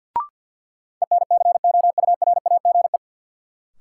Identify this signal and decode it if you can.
Morse code, heard as a keyed tone, about 53 wpm, 700 Hz. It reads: ENCOURAGE
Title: Sentences from Top 1400 Words - New Word Drill Alt - 50wpm